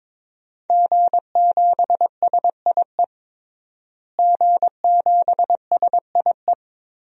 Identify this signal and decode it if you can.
Morse code, heard as a keyed tone, about 22 wpm, 700 Hz. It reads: G7SIE G7SIE
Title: DXCC-100 Call Signs - 5 Characters - Repeated 2x - 22wpm